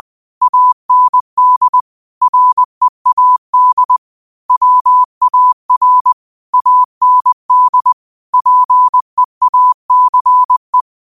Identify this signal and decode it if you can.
Morse code, heard as a keyed tone, about 20 wpm, 1000 Hz. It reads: AND READ WAR AND PEACE
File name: qotdfiles / 20wpm